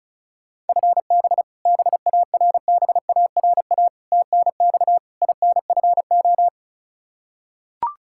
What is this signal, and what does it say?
Morse code, heard as a keyed tone, about 36 wpm, 700 Hz. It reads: FB BARBARA TNX INFO
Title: Practice QSO - Exchange 3 - More Info - 35wpm